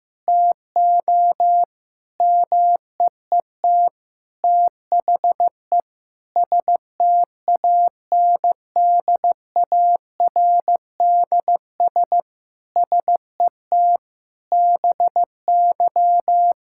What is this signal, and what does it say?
Morse code, heard as a keyed tone, about 15 wpm, 700 Hz. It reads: TO MEET THE STANDARDS SET BY
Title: Sentences from Top 1000 Words - Review Part 6 - 15wpm